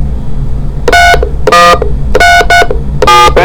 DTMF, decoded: 31337